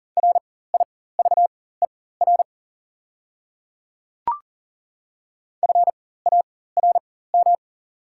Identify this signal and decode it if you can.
Morse code, heard as a keyed tone, about 40 wpm, spaced as for 19 wpm, 690 Hz. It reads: RIVER FARM
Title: 4x Character Spacing - Top 500 Words - 40wpm